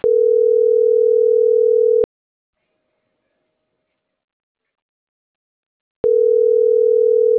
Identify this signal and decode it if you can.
Ringback tone (North America), 440 + 480 Hz.